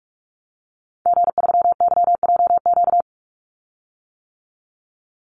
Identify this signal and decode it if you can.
Morse code, heard as a keyed tone, about 44 wpm, 700 Hz. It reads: G3YJQ
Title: Call Signs - 45wpm